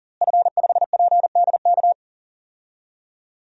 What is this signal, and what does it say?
Morse code, heard as a keyed tone, about 40 wpm, 700 Hz. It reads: F5PDK
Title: DXCC-100 Call Signs - 5 Characters - Repeated 3x - 40wpm